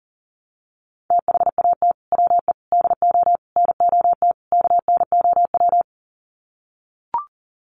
Morse code, heard as a keyed tone, about 40 wpm, 700 Hz. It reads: THAT WE DO NOT KNOW
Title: Sets of 5 Words Encore2 40wpm